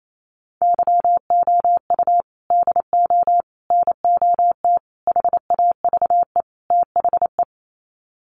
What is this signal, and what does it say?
Morse code, heard as a keyed tone, about 28 wpm, 700 Hz. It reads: YOU DO NOT HAVE THE
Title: Sets of 5 Words 28wpm